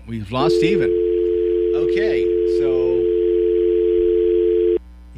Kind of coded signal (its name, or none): telephone call progress tone